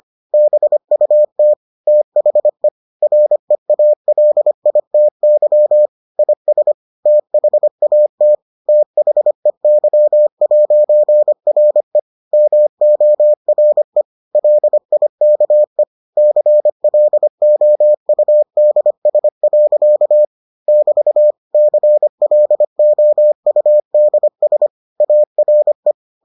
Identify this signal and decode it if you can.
Morse code, heard as a keyed tone, about 25 wpm, 600 Hz. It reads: BUT THE REALITY IS THAT THEY'RE MORE LIKE CLOUDS. <BT> CLOUDS ARE